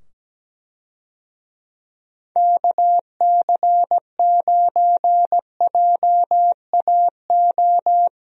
Morse code, heard as a keyed tone, about 17 wpm, 700 Hz. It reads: KC9JAO
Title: USA Call Signs - 6 Characters - 17wpm